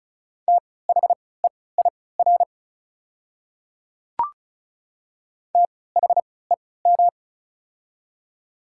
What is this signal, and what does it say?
Morse code, heard as a keyed tone, about 34 wpm, spaced as for 20 wpm, 700 Hz. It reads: THEIR THEM
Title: 3x Character Spacing - Top 200 Words - 35wpm